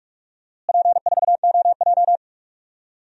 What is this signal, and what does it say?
Morse code, heard as a keyed tone, about 46 wpm, 700 Hz. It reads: P4OJ
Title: DXCC-100 Call Signs - 4 Characters - Repeated 3x - 45wpm